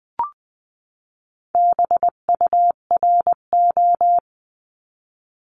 Morse code, heard as a keyed tone, about 20 wpm, 700 Hz. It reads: BURO
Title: 259 Lesson 42d - Words 18-20wpm